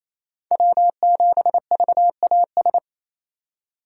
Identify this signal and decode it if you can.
Morse code, heard as a keyed tone, about 28 wpm, 700 Hz. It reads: W7VAS
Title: K1USN SST QSO Practice - 28wpm